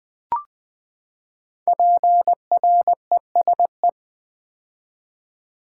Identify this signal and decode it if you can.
Morse code, heard as a keyed tone, about 20 wpm, 700 Hz. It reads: PRESE
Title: Build A Word - 5 Letters At A Time - 12-Letter Words - 20wpm